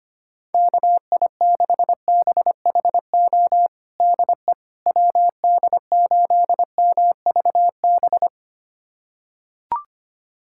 Morse code, heard as a keyed tone, about 25 wpm, 700 Hz. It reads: KI6BHO DE WD8MVB
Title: USA Callsign DE Callsign 25wpm